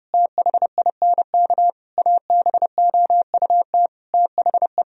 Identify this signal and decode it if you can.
Morse code, heard as a keyed tone, about 30 wpm, 700 Hz. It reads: THINK ABOUT THE
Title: Sentences from Top 1000 Words - Review Part 3 - 30wpm